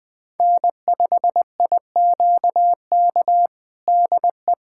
Morse code, heard as a keyed tone, about 21 wpm, 700 Hz. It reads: N5IQK DE
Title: USA Callsign DE Callsign - 5 Characters - 20wpm